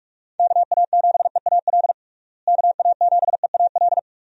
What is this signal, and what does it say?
Morse code, heard as a keyed tone, about 45 wpm, 700 Hz. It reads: KA7EAL KA7EAL
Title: USA Call Signs - 6 Characters - Repeated 3x - 45wpm